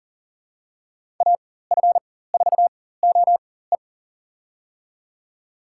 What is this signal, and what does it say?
Morse code, heard as a keyed tone, about 40 wpm, spaced as for 19 wpm, 700 Hz. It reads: AF4OE
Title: USA Call Signs - 4x Character Spacing - 40wpm